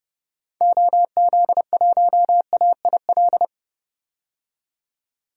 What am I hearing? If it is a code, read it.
Morse code, heard as a keyed tone, about 30 wpm, 700 Hz. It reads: OZ1AIL
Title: DXCC-100 Call Signs - 6 Characters - 30wpm